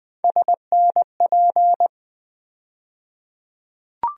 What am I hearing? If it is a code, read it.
Morse code, heard as a keyed tone, about 20 wpm, 700 Hz. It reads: SNP
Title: USA Call Sign Postfixes - 3 Letters 20wpm